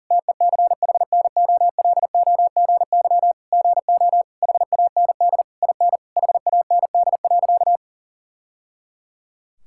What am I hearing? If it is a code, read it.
Morse code, heard as a keyed tone, about 40 wpm, 700 Hz. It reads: TECHNOLOGY GO HAND IN HAND.
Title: Sentences from Top 1000 Words - Review Part 5 - 40wpm